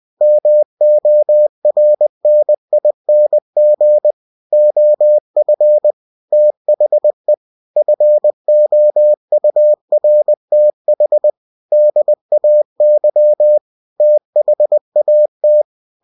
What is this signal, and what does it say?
Morse code, heard as a keyed tone, about 20 wpm, 600 Hz. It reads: MORNING OF THE FOURTH DAY THAT
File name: cw-books / winnie-the-pooh / winnie-20wpm-page-0094